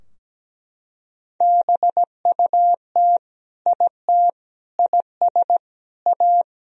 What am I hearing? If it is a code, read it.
Morse code, heard as a keyed tone, about 17 wpm, 700 Hz. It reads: BUT IT IS A
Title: Sentences from Top 1000 Words - Review Part 3 - 17wpm